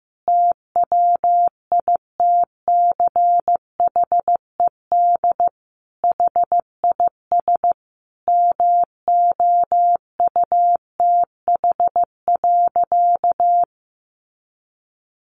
Morse code, heard as a keyed tone, about 14 wpm, 700 Hz. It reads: TWITCHED HIS MOUTH.
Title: A Sound Of Thunder 15wpm